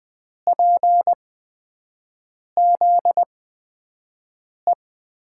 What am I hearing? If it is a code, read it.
Morse code, heard as a keyed tone, about 20 wpm, spaced as for 5 wpm, 700 Hz. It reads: PZE
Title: USA Call Signs - 8x Character Spacing - 20wpm